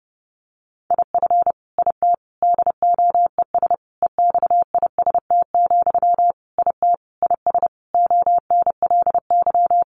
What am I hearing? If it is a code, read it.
Morse code, heard as a keyed tone, about 30 wpm, 700 Hz. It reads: IF IT DOES EXIST, IT IS ONLY
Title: Sentences from Top 1300 Words 30wpm